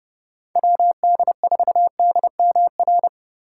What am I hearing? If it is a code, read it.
Morse code, heard as a keyed tone, about 29 wpm, 700 Hz. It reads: WD4DMR